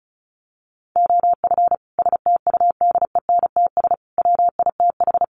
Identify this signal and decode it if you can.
Morse code, heard as a keyed tone, about 34 wpm, 700 Hz. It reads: OF STUDENTS WITH
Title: Sets of 5 Words Encore 35wpm